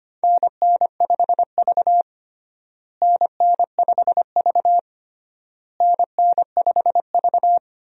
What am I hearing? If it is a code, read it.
Morse code, heard as a keyed tone, about 25 wpm, 700 Hz. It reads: NN5V NN5V NN5V
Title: USA Call Signs - 4 Characters - Repeated 3x - 25wpm